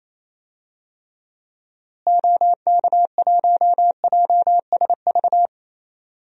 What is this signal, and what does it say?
Morse code, heard as a keyed tone, about 28 wpm, 700 Hz. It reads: OK1JSV
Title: DXCC-100 Call Signs - 6 Characters - Repeated 2x - 28wpm